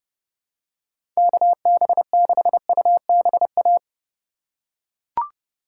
Morse code, heard as a keyed tone, about 30 wpm, 700 Hz. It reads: KB6UBA
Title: USA Call Signs - 6 Characters - 2x Character Spacing - 30wpm